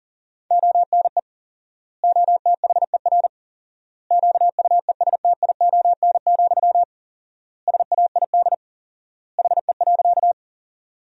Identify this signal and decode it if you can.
Morse code, heard as a keyed tone, about 40 wpm, 700 Hz. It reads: ONE OTHER QUESTION, SAID HE.
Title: 4x Word-Spacing - Sentences from Top 400 Words - 40wpm